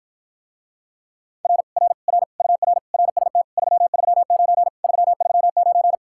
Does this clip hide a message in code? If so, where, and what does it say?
Yes: Morse code from 1.4 to 6.0 s, heard as a keyed tone, about 54 wpm, 700 Hz. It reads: R R R UR RST 339 339